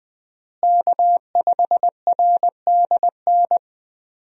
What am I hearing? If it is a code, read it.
Morse code, heard as a keyed tone, about 20 wpm, 700 Hz. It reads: K5RDN
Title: USA Call Signs - 5 Characters - Repeated 2x - 20wpm